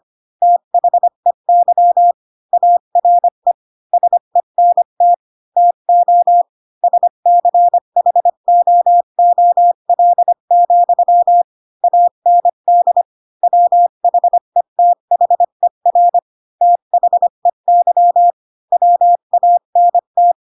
Morse code, heard as a keyed tone, about 25 wpm, 700 Hz. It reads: THEY ARE SENT TO SCHOOL, AND WHETHER THEY WANT
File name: Pinocchio / Pinocchio ch4 20 25 ik1hge